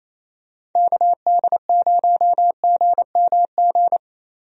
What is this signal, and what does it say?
Morse code, heard as a keyed tone, about 28 wpm, 700 Hz. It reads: KD0GMG
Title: USA Call Signs - 6 Characters - 28wpm